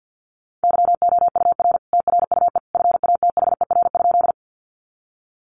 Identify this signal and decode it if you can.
Morse code, heard as a keyed tone, about 50 wpm, 700 Hz. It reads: YOUR TRUE FATHER?
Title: Sentences from Top 400 Words 50wpm